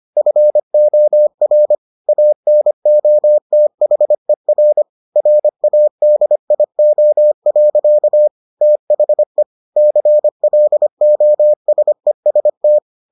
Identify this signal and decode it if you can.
Morse code, heard as a keyed tone, about 25 wpm, 600 Hz. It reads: FOR ANOTHER RADIO. THE CLOSEST